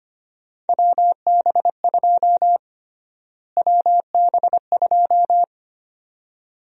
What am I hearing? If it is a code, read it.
Morse code, heard as a keyed tone, about 25 wpm, 700 Hz. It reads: WB2 WB2
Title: USA Call Sign Prefixes - Repeated 2x - 25wpm